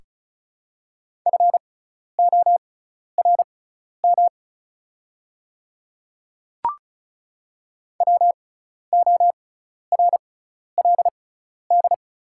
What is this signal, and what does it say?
Morse code, heard as a keyed tone, about 36 wpm, spaced as for 12 wpm, 700 Hz. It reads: FORM WORLD
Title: 6x Character Spacing - Top 300 Words - 35wpm